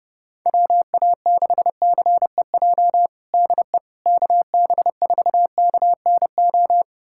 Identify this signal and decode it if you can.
Morse code, heard as a keyed tone, about 30 wpm, 700 Hz. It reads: WA6CEJ DE KB4KNO
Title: USA Callsign DE Callsign - 6 Characters - 30wpm